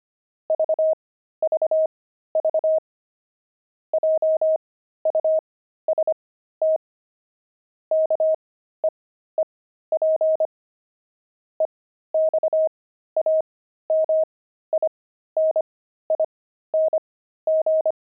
Morse code, heard as a keyed tone, about 25 wpm, spaced as for 13 wpm, 630 Hz. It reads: VVV JUST KEEP EXAMINING
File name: pangram-fs-13-cs-25-low